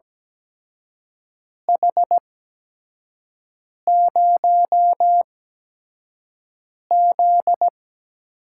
Morse code, heard as a keyed tone, about 17 wpm, spaced as for 5 wpm, 700 Hz. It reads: H0Z
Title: USA Call Signs - 4 Characters - 8x Character Spacing - 17wpm